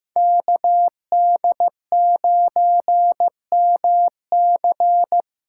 Morse code, heard as a keyed tone, about 15 wpm, 700 Hz. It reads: KD9MC
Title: USA Call Signs - 5 Characters - 3x Character Spacing - 15wpm